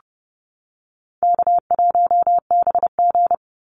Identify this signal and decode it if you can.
Morse code, heard as a keyed tone, about 30 wpm, 700 Hz. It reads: K1BG
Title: Call Signs - 30wpm